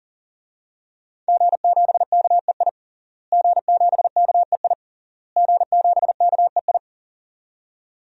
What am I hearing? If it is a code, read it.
Morse code, heard as a keyed tone, about 40 wpm, 700 Hz. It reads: G7KEI G7KEI G7KEI
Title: DXCC-100 Call Signs - 5 Characters - Repeated 3x - 40wpm